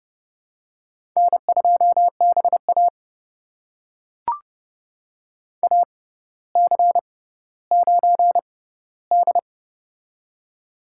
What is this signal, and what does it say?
Morse code, heard as a keyed tone, about 30 wpm, spaced as for 15 wpm, 690 Hz. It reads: N2BA AC9D
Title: USA Call Signs - 4 Characters - 6x Character Spacing - 30wpm